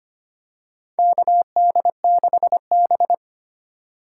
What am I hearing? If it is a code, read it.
Morse code, heard as a keyed tone, about 25 wpm, 700 Hz. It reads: KD6B
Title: USA Call Signs - 4 Characters - 2x Character Spacing - 25wpm